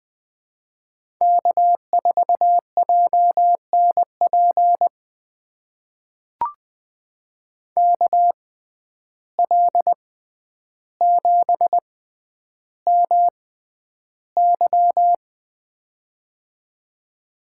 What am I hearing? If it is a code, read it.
Morse code, heard as a keyed tone, about 20 wpm, spaced as for 10 wpm, 700 Hz. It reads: K4JNP KL7MY
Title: USA Call Signs - 5 Characters - 6x Character Spacing - 20wpm